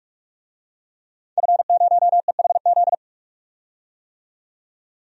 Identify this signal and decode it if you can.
Morse code, heard as a keyed tone, about 44 wpm, 690 Hz. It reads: F0EHZ
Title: DXCC-100 Call Signs - 5 Characters - 45wpm